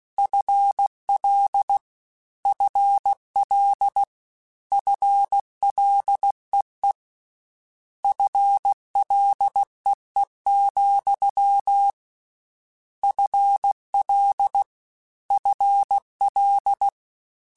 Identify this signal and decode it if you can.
Morse code, heard as a keyed tone, about 16 wpm, 800 Hz. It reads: FL FL FLEE FLEE, FL FL
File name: Combinations / Combinations-mp3s / Combinations@15